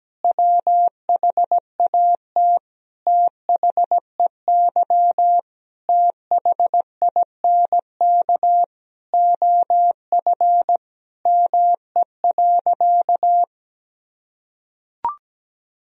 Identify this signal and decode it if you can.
Morse code, heard as a keyed tone, about 17 wpm, 700 Hz. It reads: WHAT THEY THINK OF ME.
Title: Sentences from Top 1000 Words - Review Part 1 - 17wpm